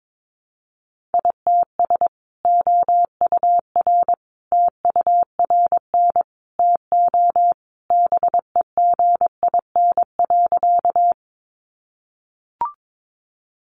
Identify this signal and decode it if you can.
Morse code, heard as a keyed tone, about 22 wpm, 700 Hz. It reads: ITS OUR TURN TO BEGIN.
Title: Sentences from Top 600 Words 22wpm